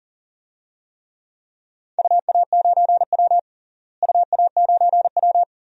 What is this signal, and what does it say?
Morse code, heard as a keyed tone, about 39 wpm, 700 Hz. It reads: UA9W UA9W
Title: DXCC-100 Call Signs - 4 Characters - Repeated 2x - 40wpm